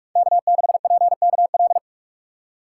Morse code, heard as a keyed tone, about 59 wpm, spaced as for 52 wpm, 700 Hz. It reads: K6PKL
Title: Speed Racing - USA 1x3 Call Signs 45wpm